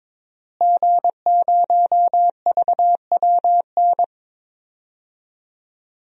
Morse code, heard as a keyed tone, about 22 wpm, 700 Hz. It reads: G0VWN
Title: DXCC-100 Call Signs - 5 Characters - Repeated 2x - 22wpm